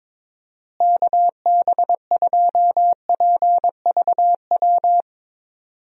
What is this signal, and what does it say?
Morse code, heard as a keyed tone, about 21 wpm, 700 Hz. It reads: KB2PVW
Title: USA Call Signs - 6 Characters - Repeated 2x - 22wpm